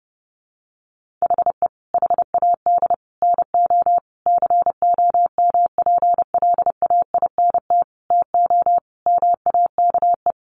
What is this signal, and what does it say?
Morse code, heard as a keyed tone, about 30 wpm, 700 Hz. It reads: HE HAD NO COMPLAINT TO MAKE
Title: Sentences from Top 1400 Words 30wpm